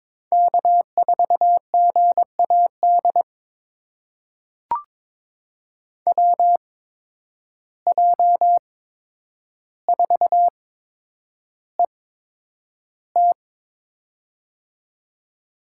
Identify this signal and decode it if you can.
Morse code, heard as a keyed tone, about 22 wpm, spaced as for 9 wpm, 700 Hz. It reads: K4GAD WJ4ET
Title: USA Call Signs - 5 Characters - 8x Character Spacing - 22wpm